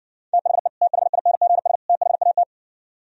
Morse code, heard as a keyed tone, about 64 wpm, 700 Hz. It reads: THE THINGS THAT